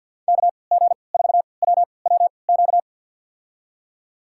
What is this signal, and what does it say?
Morse code, heard as a keyed tone, about 52 wpm, spaced as for 29 wpm, 700 Hz. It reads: KG4WWQ
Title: USA Call Signs - 3x Character Spacing - 50wpm